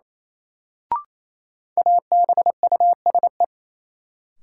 Morse code, heard as a keyed tone, about 28 wpm, 700 Hz. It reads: ABUSE
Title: Sentences from Top 1300 Words - New Word Drill Alt - 28wpm